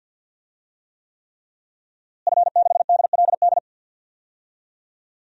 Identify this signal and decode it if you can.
Morse code, heard as a keyed tone, about 50 wpm, 700 Hz. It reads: F6DLD
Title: DXCC-100 Call Signs - 5 Characters - Repeated 3x - 50wpm